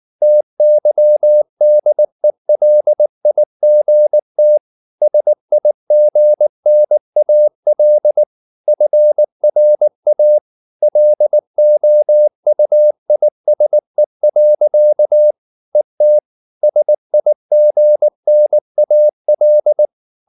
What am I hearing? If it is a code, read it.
Morse code, heard as a keyed tone, about 19 wpm, 600 Hz. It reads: TYDELIGT SIGNAL FRA LOUISE. ET SIGNAL
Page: Ferie 19 wpm